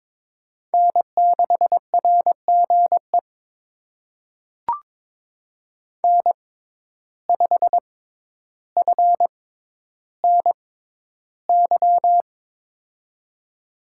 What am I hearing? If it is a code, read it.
Morse code, heard as a keyed tone, about 23 wpm, spaced as for 11 wpm, 700 Hz. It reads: N6RGE N5FNY
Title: USA Call Signs - 5 Characters - 6x Character Spacing - 22wpm